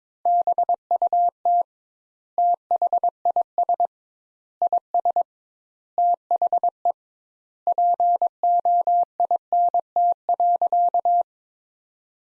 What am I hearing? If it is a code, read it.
Morse code, heard as a keyed tone, about 22 wpm, 700 Hz. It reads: BUT THIS IS THE POINT.
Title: Sentences Top 200 Words - 22wpm